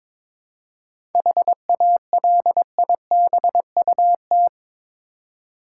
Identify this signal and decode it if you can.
Morse code, heard as a keyed tone, about 22 wpm, 700 Hz. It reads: HALIBUT
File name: Difficult Words with Hints - 22wpm